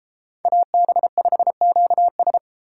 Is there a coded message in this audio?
Yes: Morse code from 0.5 to 2.4 s, heard as a keyed tone, about 34 wpm, 700 Hz. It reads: AB5QS